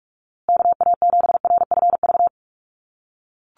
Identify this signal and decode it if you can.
Morse code, heard as a keyed tone, about 44 wpm, 700 Hz. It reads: KA7RFV
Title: USA Call Signs Repeated 2x 45wpm